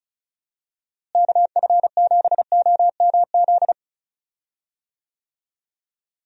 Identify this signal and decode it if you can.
Morse code, heard as a keyed tone, about 36 wpm, 700 Hz. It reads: KF7OMZ
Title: USA Call Signs - 6 Characters - Repeated 2x - 35wpm